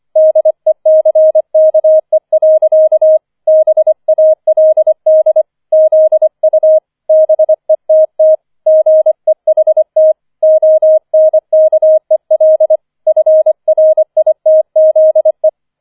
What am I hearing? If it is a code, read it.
Morse code, heard as a keyed tone, about 25 wpm, 630 Hz. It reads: DECKE. BALD ZU BETT GEHT ONKEL FRITZE
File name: 005 Fünfter Streich 25wpm